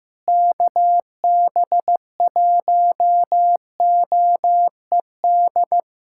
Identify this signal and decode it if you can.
Morse code, heard as a keyed tone, about 15 wpm, 700 Hz. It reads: KB1OED
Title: USA Callsign DE Callsign - 6 Characters - 15wpm